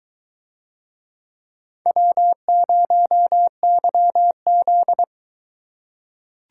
Morse code, heard as a keyed tone, about 23 wpm, 700 Hz. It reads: W0YZ